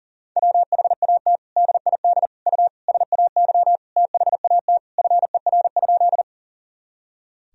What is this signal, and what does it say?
Morse code, heard as a keyed tone, about 40 wpm, 700 Hz. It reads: WHAT DID U SAY THAT FER?
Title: Sentences from Top 200 Words and Abbreviations 40wpm